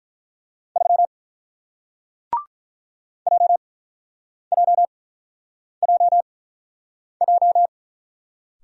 Morse code, heard as a keyed tone, about 46 wpm, spaced as for 9 wpm, 700 Hz. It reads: 3 JJJJ